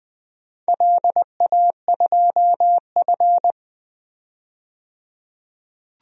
Morse code, heard as a keyed tone, about 20 wpm, 700 Hz. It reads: LA2F
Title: DXCC-100 Call Signs - 4 Characters - 20wpm